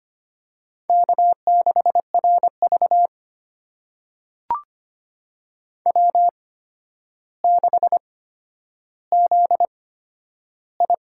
Morse code, heard as a keyed tone, about 25 wpm, spaced as for 11 wpm, 700 Hz. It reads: K6RV W6ZI
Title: USA Call Signs - 4 Characters - 8x Character Spacing - 25wpm